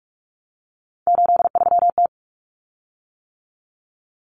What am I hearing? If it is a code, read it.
Morse code, heard as a keyed tone, about 44 wpm, 700 Hz. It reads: 83T
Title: Common Cut Numbers - 3-Digit 45wpm